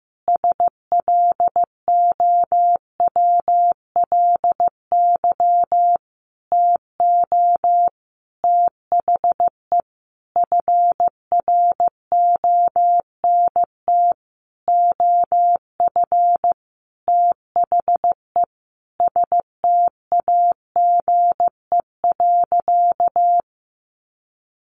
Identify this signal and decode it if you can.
Morse code, heard as a keyed tone, about 15 wpm, 700 Hz. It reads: SLOWLY TO THE FRONT OF THE STAGE.
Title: Sentences from Top 1400 Words 15wpm